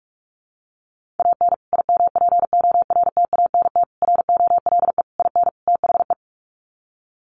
Morse code, heard as a keyed tone, about 44 wpm, 700 Hz. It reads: AN IMPORTANT ROLE IN THE
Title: Sets of 5 Words Encore2 45wpm